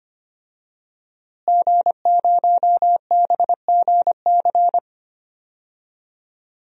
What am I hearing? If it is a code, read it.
Morse code, heard as a keyed tone, about 25 wpm, 700 Hz. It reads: G0BGC
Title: DXCC-100 Call Signs - 5 Characters - Repeated 2x - 25wpm